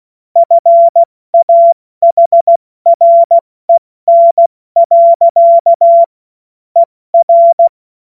Morse code, heard as a keyed tone, about 16 wpm, 680 Hz. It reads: FAHREN. ER
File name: Morsekurs / Witze / Witz 13. Januar 2026 70 F80